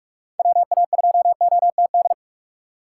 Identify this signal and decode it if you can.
Morse code, heard as a keyed tone, about 46 wpm, 700 Hz. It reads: WA2OTD
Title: USA Call Signs - 2x Character Spacing - 45wpm